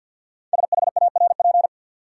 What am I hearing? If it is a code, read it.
Morse code, heard as a keyed tone, about 48 wpm, 700 Hz. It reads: SHARP
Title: Sentences from Top 1600 Words - New Word Drill - 50wpm